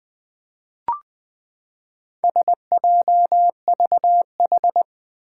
Morse code, heard as a keyed tone, about 20 wpm, 700 Hz. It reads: SJVH
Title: Four Challenging Letter-Number Combinations 20wpm